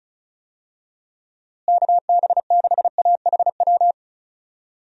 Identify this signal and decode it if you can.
Morse code, heard as a keyed tone, about 34 wpm, 700 Hz. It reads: KB6AHW
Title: USA Call Signs - 6 Characters - Repeated 3x - 35wpm